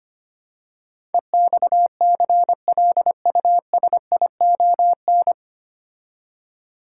Morse code, heard as a keyed tone, about 25 wpm, 700 Hz. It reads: EXCLUSION
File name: practice / 25wpm / 9 Letter Words - First 3 Letters Given - 25wpm